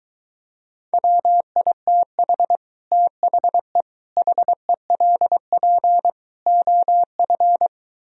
Morse code, heard as a keyed tone, about 22 wpm, 700 Hz. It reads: WITH THE HELP OF